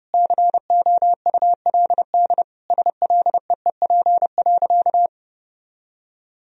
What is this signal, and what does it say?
Morse code, heard as a keyed tone, about 32 wpm, 700 Hz. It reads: COULD SLEEP.
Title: Sentences from Top 500 Words and Abbreviations 30wpm